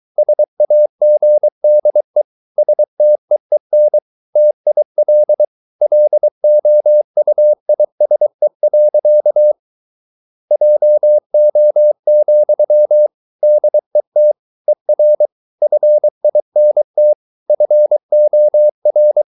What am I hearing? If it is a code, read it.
Morse code, heard as a keyed tone, about 23 wpm, 600 Hz. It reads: SAGDE STEEN TIL LOUISE. JO, DET ER FINT FOR
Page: Ferie 23 wpm